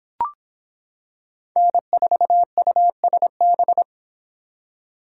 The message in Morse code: N4USB

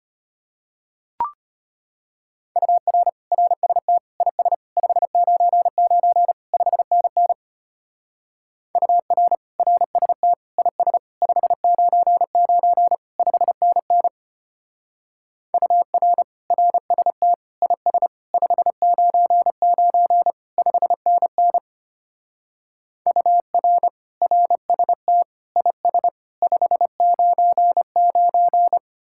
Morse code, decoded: UR RST IS 599 5NN UR RST IS 599 5NN UR RST IS 599 5NN UR RST IS 599

